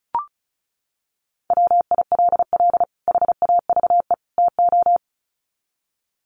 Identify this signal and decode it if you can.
Morse code, heard as a keyed tone, about 34 wpm, 700 Hz. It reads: WILL HAVE TO
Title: Sets of 3 Words 35wpm